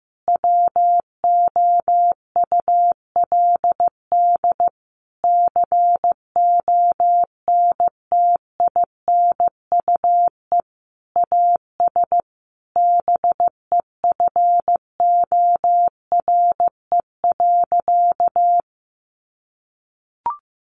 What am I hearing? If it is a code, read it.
Morse code, heard as a keyed tone, about 15 wpm, 700 Hz. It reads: WOULD CONTINUE AS BEFORE.
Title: Sentences from Top 1400 Words 15wpm